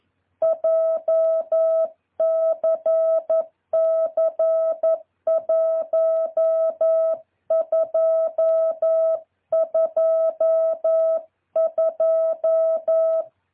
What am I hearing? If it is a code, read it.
Morse code, heard as a keyed tone, about 11 wpm, 650 Hz. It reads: JCC 1222
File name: jh8cbh / 2023cw